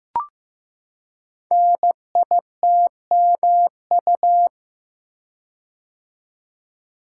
Morse code, heard as a keyed tone, about 15 wpm, 700 Hz. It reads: NITMU
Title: Code Groups - 5 Letter:Numbers - 15wpm